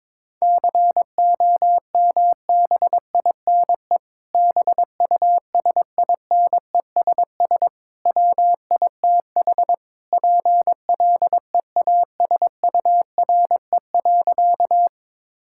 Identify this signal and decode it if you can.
Morse code, heard as a keyed tone, about 22 wpm, 700 Hz. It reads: COMBINE BUSINESS WITH PLEASURE.
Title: Sentences from Top 1600 Words 22wpm